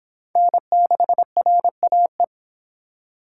Morse code, heard as a keyed tone, about 24 wpm, 700 Hz. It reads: N6RAE